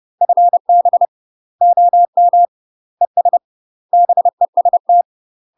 Morse code, heard as a keyed tone, about 30 wpm, 700 Hz. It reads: FB OM ES BEST
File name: UK MASTER QSOs TRAINING / 30WPM / UK MASTER QSOs 0041